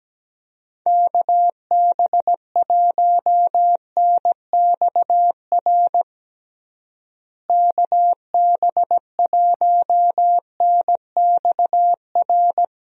Morse code, heard as a keyed tone, about 17 wpm, 700 Hz. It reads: KB1NXR KB1NXR
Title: USA Call Signs - 6 Characters - Repeated 3x - 17wpm